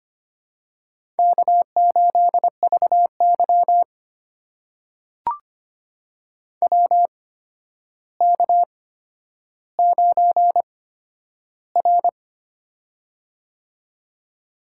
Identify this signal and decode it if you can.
Morse code, heard as a keyed tone, about 25 wpm, spaced as for 11 wpm, 700 Hz. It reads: K8VY WK9R
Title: USA Call Signs - 4 Characters - 8x Character Spacing - 25wpm